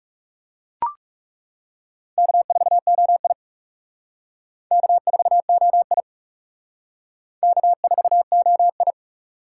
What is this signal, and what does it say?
Morse code, heard as a keyed tone, about 36 wpm, 700 Hz. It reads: K4OI K4OI K4OI
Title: Speed Racing - USA 1x2 Call Signs 30wpm